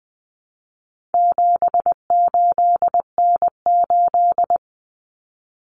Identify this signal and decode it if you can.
Morse code, heard as a keyed tone, about 20 wpm, 700 Hz. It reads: 78N8
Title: Common Cut Numbers - 4-Digit 20wpm